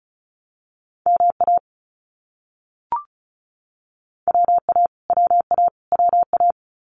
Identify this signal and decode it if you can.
Morse code, heard as a keyed tone, about 34 wpm, 700 Hz. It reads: MA WA WA WA
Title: US States Repeated 3x 35wpm